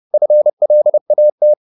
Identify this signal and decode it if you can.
Morse code, heard as a keyed tone, about 30 wpm, 600 Hz. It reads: FLAT